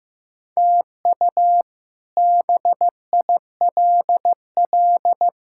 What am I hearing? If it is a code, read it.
Morse code, heard as a keyed tone, about 15 wpm, 700 Hz. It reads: TU BILL
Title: K1USN SST QSO Practice - 15wpm